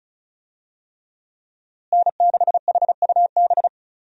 Morse code, heard as a keyed tone, about 36 wpm, 700 Hz. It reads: N6HUB
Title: USA Call Signs - 5 Characters - Repeated 3x - 35wpm